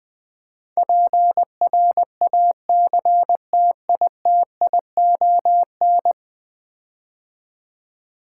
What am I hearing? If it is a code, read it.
Morse code, heard as a keyed tone, about 21 wpm, 700 Hz. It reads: PRACTITION
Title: Build A Word - 5 Letters At A Time - 12-Letter Words - 20wpm